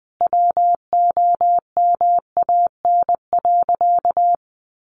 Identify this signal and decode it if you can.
Morse code, heard as a keyed tone, about 19 wpm, 700 Hz. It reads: WOMAN.
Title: Sentences from Top 1100 Words 20wpm